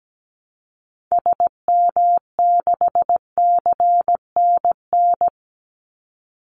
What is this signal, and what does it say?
Morse code, heard as a keyed tone, about 17 wpm, 700 Hz. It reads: SM6CNN
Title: Call Signs Repeated 3x - 17wpm